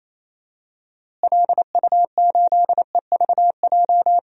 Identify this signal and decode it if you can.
Morse code, heard as a keyed tone, about 28 wpm, 700 Hz. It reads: LU8EVJ
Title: DXCC-100 Call Signs - 6 Characters - Repeated 2x - 28wpm